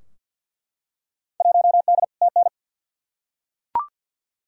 Morse code, heard as a keyed tone, about 50 wpm, 700 Hz. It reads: 1D TN
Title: Field Day Exchanges - 50wpm